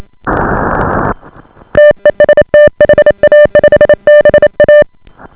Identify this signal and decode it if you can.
Morse code, heard as a keyed tone, about 28 wpm, 590 Hz. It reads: TEST HA5BA